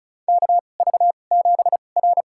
Morse code, heard as a keyed tone, about 36 wpm, spaced as for 25 wpm, 700 Hz. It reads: KV7R
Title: USA Call Signs - 4 Characters - 2x Character Spacing - 35wpm